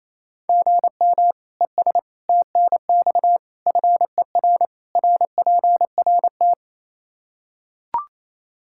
Morse code, heard as a keyed tone, about 27 wpm, 700 Hz. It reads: GM ES TNX FER RPRT
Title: K1USN SST QSO Practice 28wpm